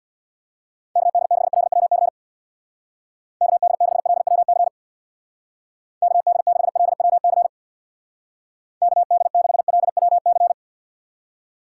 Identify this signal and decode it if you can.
Morse code, heard as a keyed tone, about 59 wpm, 700 Hz. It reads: KD6LWC KD6LWC KD6LWC KD6LWC
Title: Speed Racing - USA 2x3 Call Signs 50wpm